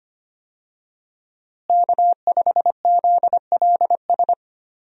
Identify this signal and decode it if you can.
Morse code, heard as a keyed tone, about 25 wpm, 700 Hz. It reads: K5ZLS
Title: USA Call Signs - 5 Characters - Repeated 3x - 25wpm